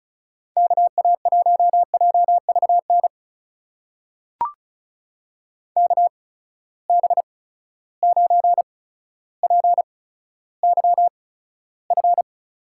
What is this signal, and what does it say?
Morse code, heard as a keyed tone, about 36 wpm, spaced as for 15 wpm, 700 Hz. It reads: KA1JVN KB9PYF
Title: USA Call Signs - 6 Characters - 8x Character Spacing - 35wpm